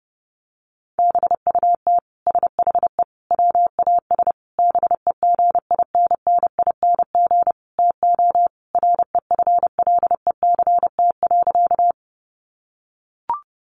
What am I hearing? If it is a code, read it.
Morse code, heard as a keyed tone, about 29 wpm, 700 Hz. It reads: BUT SHE WAS BEGINNING TO REFLECT.
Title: Sentences from Top 1500 Words 30wpm